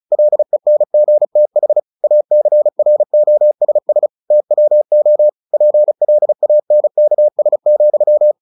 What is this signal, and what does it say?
Morse code, heard as a keyed tone, about 34 wpm, 600 Hz. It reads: LENGTH ACROSS TWO PLANKS,
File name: the old man and the sea / the old man and the sea 1 35wpm